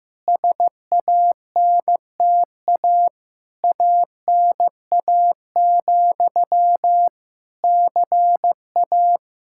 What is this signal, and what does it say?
Morse code, heard as a keyed tone, about 15 wpm, 700 Hz. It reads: SANTA ANA, CA
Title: Practice QSO - Exchange 1 - Beginning - 15wpm